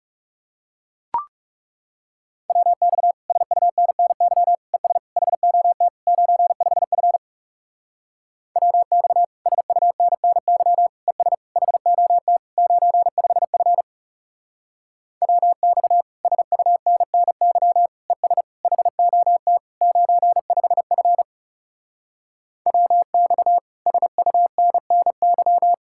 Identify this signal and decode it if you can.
Morse code, heard as a keyed tone, about 40 wpm, 700 Hz. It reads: WX SUNNY ES HOT 95F WX SUNNY ES HOT 95F WX SUNNY ES HOT 95F WX SUNNY